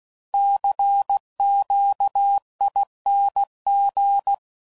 Morse code, heard as a keyed tone, about 16 wpm, 800 Hz. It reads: CQING
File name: Suffixes / Special INGs / Sngl Ings